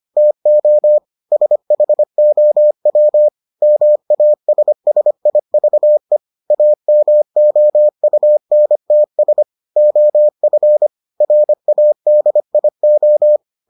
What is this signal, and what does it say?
Morse code, heard as a keyed tone, about 25 wpm, 600 Hz. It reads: TO SHOW MASSIVE AMOUNTS OF RADIO